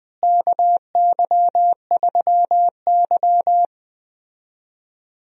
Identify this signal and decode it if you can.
Morse code, heard as a keyed tone, about 21 wpm, 700 Hz. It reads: KY3Y
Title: USA Call Signs - 4 Characters - Repeated 2x - 20wpm